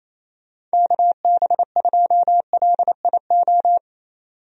Morse code, heard as a keyed tone, about 28 wpm, 700 Hz. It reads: KB2LIO
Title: USA Call Signs - 6 Characters - 28wpm